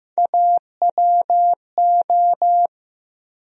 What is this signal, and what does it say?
Morse code, heard as a keyed tone, about 15 wpm, 700 Hz. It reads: AWO
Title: USA Call Sign Postfixes - 3 Letters 15wpm